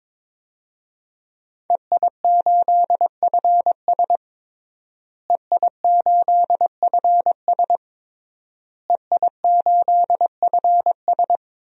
Morse code, heard as a keyed tone, about 22 wpm, 700 Hz. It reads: EI8FS EI8FS EI8FS
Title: DXCC-100 Call Signs - 5 Characters - Repeated 3x - 22wpm